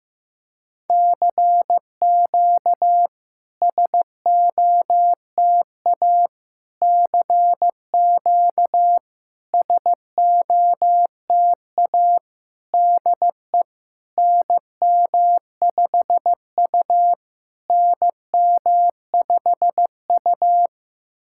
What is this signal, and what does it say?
Morse code, heard as a keyed tone, about 15 wpm, 690 Hz. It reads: CQ SOTA CQ SOTA DE NM5U NM5U
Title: SOTA QSO Practice 15wpm